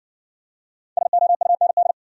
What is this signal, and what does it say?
Morse code, heard as a keyed tone, about 60 wpm, 700 Hz. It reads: SOUND